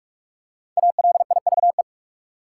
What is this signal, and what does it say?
Morse code, heard as a keyed tone, about 42 wpm, 700 Hz. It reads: ALIVE
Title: Five Letter Words - 45wpm